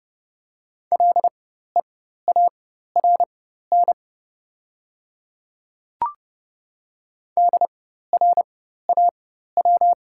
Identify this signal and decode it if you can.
Morse code, heard as a keyed tone, about 30 wpm, spaced as for 14 wpm, 700 Hz. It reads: LEARN DRAW
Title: 4x Character Spacing - Top 500 Words - 30wpm